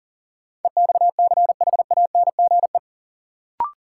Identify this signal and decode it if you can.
Morse code, heard as a keyed tone, about 40 wpm, 700 Hz. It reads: EXCHANGE
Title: Sets of 5 Words - New Word Drill - 40wpm